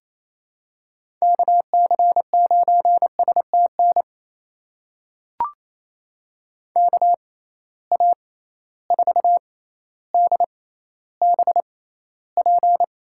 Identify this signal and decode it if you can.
Morse code, heard as a keyed tone, about 28 wpm, spaced as for 14 wpm, 700 Hz. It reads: KC9STN KA4DBP
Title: USA Call Signs - 6x Character Spacing - 28wpm